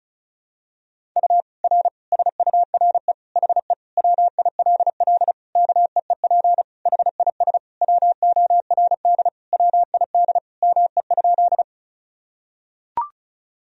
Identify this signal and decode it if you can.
Morse code, heard as a keyed tone, about 36 wpm, 700 Hz. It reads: U R SURE HE WILL KEEP HIS WORD WID ME?
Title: Sentences from Top 300 Words and Abbreviations 35wpm